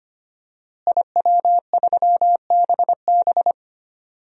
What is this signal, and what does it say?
Morse code, heard as a keyed tone, about 25 wpm, 700 Hz. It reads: IW3BB
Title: DXCC-100 Call Signs - 5 Characters - Repeated 3x - 25wpm